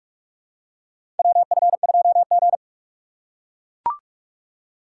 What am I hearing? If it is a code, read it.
Morse code, heard as a keyed tone, about 44 wpm, 690 Hz. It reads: WF2G
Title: USA Call Signs - 4 Characters - 3x Character Spacing - 45wpm